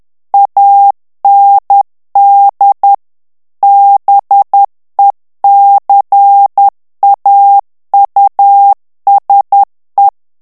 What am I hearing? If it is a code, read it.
Morse code, heard as a keyed tone, about 10 wpm, 800 Hz. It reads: AND BECAUSE
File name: ANGLAIS texte clair 3 vitesse 10 mots minute